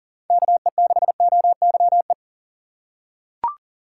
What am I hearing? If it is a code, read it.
Morse code, heard as a keyed tone, about 40 wpm, 700 Hz. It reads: KE6OYE
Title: USA Call Signs - 2x Character Spacing - 40wpm